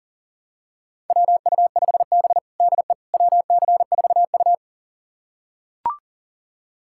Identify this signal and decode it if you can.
Morse code, heard as a keyed tone, about 40 wpm, 700 Hz. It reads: WU5B DE WC4U
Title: USA Callsign DE Callsign - 4 Characters - 40wpm